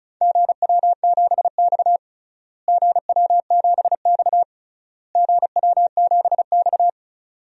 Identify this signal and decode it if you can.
Morse code, heard as a keyed tone, about 34 wpm, 700 Hz. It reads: GW7X GW7X GW7X
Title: DXCC-100 Call Signs - 4 Characters - Repeated 3x - 35wpm